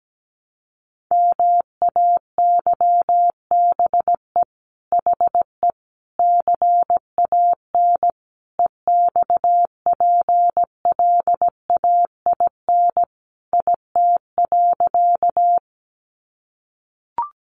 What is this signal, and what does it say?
Morse code, heard as a keyed tone, about 17 wpm, 700 Hz. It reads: MAYBE HE CAN EXPLAIN IT.
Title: Sentences from Top 1100 Words 17wpm